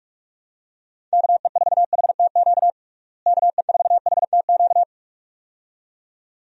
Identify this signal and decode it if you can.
Morse code, heard as a keyed tone, about 42 wpm, 700 Hz. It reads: KE4HTQ KE4HTQ
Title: USA Call Signs - 6 Characters - Repeated 2x - 45wpm